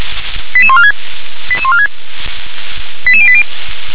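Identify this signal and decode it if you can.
Five-tone selective call (ZVEI), heard as quick runs of five tones, tones of about 70 ms. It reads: 88146 88146 88080
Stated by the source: ZVEI five-tone selective call, received on SDR.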